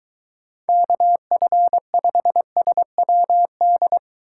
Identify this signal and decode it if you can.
Morse code, heard as a keyed tone, about 22 wpm, 700 Hz. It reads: KF5SWD